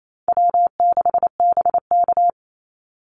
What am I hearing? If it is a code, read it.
Morse code, heard as a keyed tone, about 28 wpm, 700 Hz. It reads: W6BK
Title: Easy Call Signs Repeated 3x - 28wpm